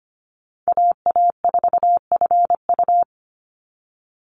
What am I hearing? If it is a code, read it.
Morse code, heard as a keyed tone, about 25 wpm, 700 Hz. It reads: AA4FU
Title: Call Signs Repeated 2x - 25wpm